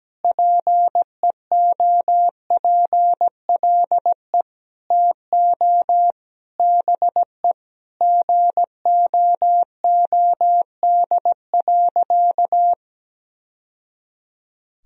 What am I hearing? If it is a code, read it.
Morse code, heard as a keyed tone, about 17 wpm, 700 Hz. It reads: PEOPLE TO BE GOOD.
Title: Sentences from Top 1000 Words - Review Part 3 - 17wpm